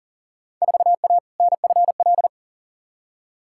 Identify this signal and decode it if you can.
Morse code, heard as a keyed tone, about 40 wpm, 700 Hz. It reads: 4A NFL
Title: Field Day Exchanges - 40wpm